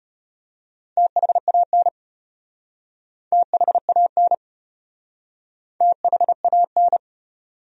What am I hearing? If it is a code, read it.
Morse code, heard as a keyed tone, about 33 wpm, 700 Hz. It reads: THAN THAN THAN